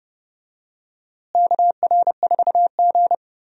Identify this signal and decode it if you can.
Morse code, heard as a keyed tone, about 31 wpm, 700 Hz. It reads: KR4G
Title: USA Call Signs - 4 Characters - Repeated 3x - 30wpm